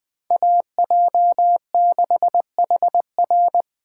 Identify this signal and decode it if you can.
Morse code, heard as a keyed tone, about 24 wpm, 700 Hz. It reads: AJ6HR
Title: Speed Racing - USA 2x2 Call Signs 20wpm